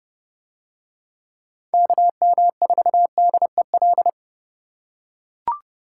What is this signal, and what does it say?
Morse code, heard as a keyed tone, about 30 wpm, 700 Hz. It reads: KM4DEL